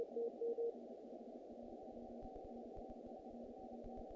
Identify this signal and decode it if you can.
Morse code, heard as a keyed tone, about 33 wpm, 240 Hz. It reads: DE N5RP N5RP
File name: contests / naqp-cw-k5tr-op-k5pi- -1800